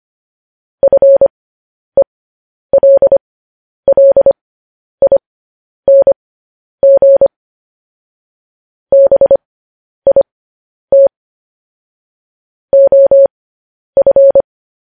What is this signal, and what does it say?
Morse code, heard as a keyed tone, about 25 wpm, spaced as for 10 wpm, 550 Hz. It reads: FELLING BIT OF
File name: podcast / News headlines-1x-25x10